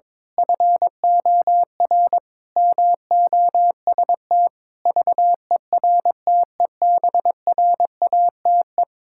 Morse code, heard as a keyed tone, about 21 wpm, 700 Hz. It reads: FOR MOST VERTEBRATE
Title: Pangram Challenge v2 - 22wpm